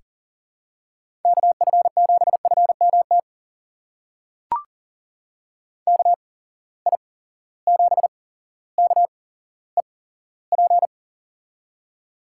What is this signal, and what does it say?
Morse code, heard as a keyed tone, about 37 wpm, spaced as for 17 wpm, 700 Hz. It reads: KF7FMT KI7KEP
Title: USA Call Signs - 6 Characters - 8x Character Spacing - 40wpm